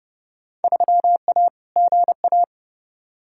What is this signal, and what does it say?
Morse code, heard as a keyed tone, about 31 wpm, 700 Hz. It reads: 3A GA
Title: Field Day Exchanges - 30wpm